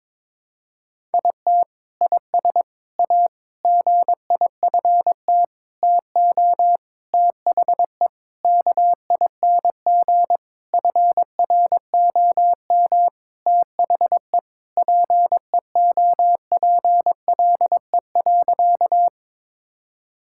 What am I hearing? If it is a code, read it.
Morse code, heard as a keyed tone, about 22 wpm, 700 Hz. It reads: IT IS A GIFT TO THE KING FROM THE PEOPLE.
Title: Sentences from Top 1000 Words - Review Part 6 - 22wpm